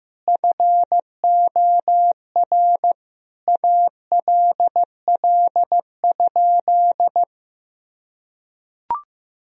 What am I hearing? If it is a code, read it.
Morse code, heard as a keyed tone, about 15 wpm, 700 Hz. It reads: FOR ALL?
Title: Sentences from Top 1000 Words - Review Part 3 - 15wpm